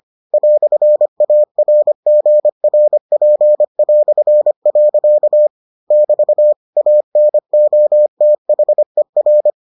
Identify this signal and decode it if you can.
Morse code, heard as a keyed tone, about 25 wpm, 600 Hz. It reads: "ARGRP". <BT> ANOTHER